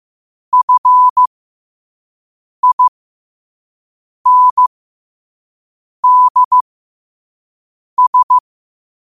Morse code, heard as a keyed tone, about 15 wpm, spaced as for 5 wpm, 1000 Hz. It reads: FINDS
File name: qotdfiles / 5wpm